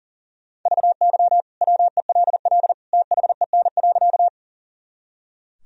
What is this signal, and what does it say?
Morse code, heard as a keyed tone, about 40 wpm, 700 Hz. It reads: VY WELL THEN.
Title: Sentences from Top 200 Words and Abbreviations 40wpm